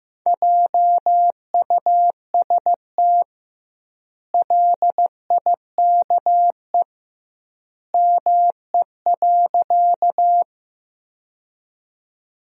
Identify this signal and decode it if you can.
Morse code, heard as a keyed tone, about 15 wpm, 700 Hz. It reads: JUST LIKE ME.
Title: 2x Word-Spacing - Sentences from Top 200 Words - 15wpm